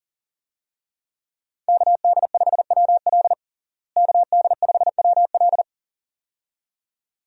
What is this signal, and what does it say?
Morse code, heard as a keyed tone, about 40 wpm, 700 Hz. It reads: KD5WL KD5WL
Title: USA Call Signs - 5 Characters - Repeated 2x - 40wpm